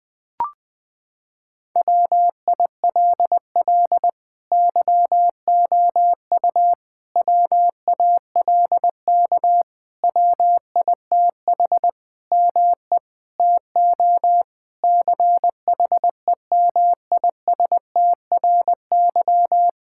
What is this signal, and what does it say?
Morse code, heard as a keyed tone, about 21 wpm, 700 Hz. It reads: WILL YOU WALK WITH ME TO CHEMISTRY
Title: Sentences from Top 1000 Words - Review Part 6 - 20wpm